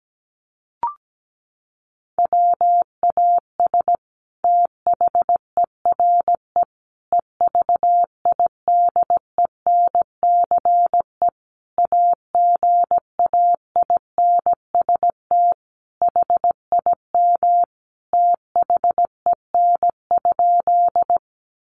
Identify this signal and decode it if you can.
Morse code, heard as a keyed tone, about 17 wpm, 700 Hz. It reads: WAS THERE EVIDENCE AGAINST HIM THEN?
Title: Sentences from Top 1300 Words 17wpm